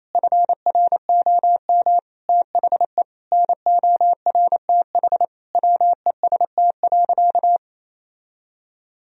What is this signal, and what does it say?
Morse code, heard as a keyed tone, about 27 wpm, 700 Hz. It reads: FROM THE NORTH WEST.
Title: Sentences from Top 1000 Words - Review Part 2 - 28wpm